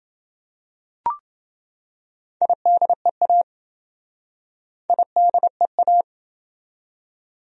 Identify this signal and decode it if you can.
Morse code, heard as a keyed tone, about 27 wpm, 700 Hz. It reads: IDEA IDEA